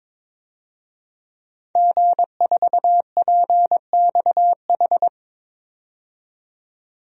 Morse code, heard as a keyed tone, about 21 wpm, 700 Hz. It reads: G4PXH
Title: DXCC-100 Call Signs - 5 Characters - 22wpm